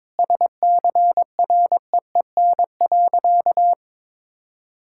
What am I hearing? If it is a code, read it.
Morse code, heard as a keyed tone, about 22 wpm, 700 Hz. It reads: SCREEN.
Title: Sentences from Top 1000 Words - Review Part 6 - 22wpm